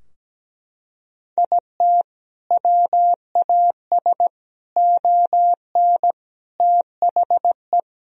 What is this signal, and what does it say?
Morse code, heard as a keyed tone, about 17 wpm, 700 Hz. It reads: IT WAS ON THE
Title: Sentences from Top 1000 Words - Review Part 2 - 17wpm